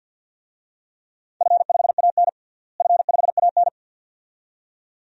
Morse code, heard as a keyed tone, about 49 wpm, 700 Hz. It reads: F5AN F5AN
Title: DXCC-100 Call Signs - 4 Characters - Repeated 2x - 50wpm